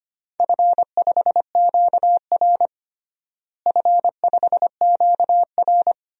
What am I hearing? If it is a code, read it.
Morse code, heard as a keyed tone, about 25 wpm, 700 Hz. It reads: F5QR F5QR
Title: DXCC-100 Call Signs - 4 Characters - Repeated 2x - 25wpm